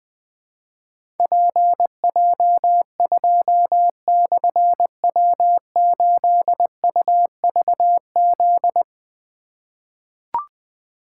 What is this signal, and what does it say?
Morse code, heard as a keyed tone, about 20 wpm, 700 Hz. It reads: PJ2/W8UVZ
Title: Difficult Call Signs 20wpm